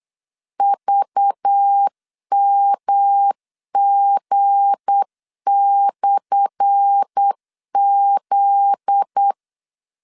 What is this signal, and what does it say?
Morse code, heard as a keyed tone, about 8 wpm, 800 Hz. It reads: VMG/Z
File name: LEZIONI / 10 SIGNS / 10A SLASH 8wpm 3452